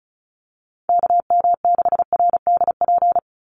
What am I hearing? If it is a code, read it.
Morse code, heard as a keyed tone, about 34 wpm, 700 Hz. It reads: KM6RDP
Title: USA Call Signs 35wpm